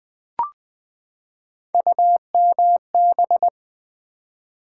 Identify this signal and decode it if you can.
Morse code, heard as a keyed tone, about 20 wpm, 700 Hz. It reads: UMB
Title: Build A Word - 3 Letters At A Time - 8-Letter Words - 20wpm